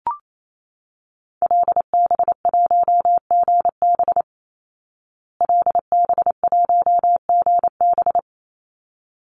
Morse code, heard as a keyed tone, about 28 wpm, 700 Hz. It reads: LB1GB LB1GB
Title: Call Signs Repeated 3x - 28wpm